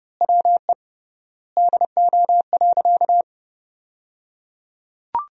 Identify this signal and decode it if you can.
Morse code, heard as a keyed tone, about 30 wpm, 700 Hz. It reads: WE DO.
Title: 3x Word-Spacing - Sentences from Top 100 Words - 30wpm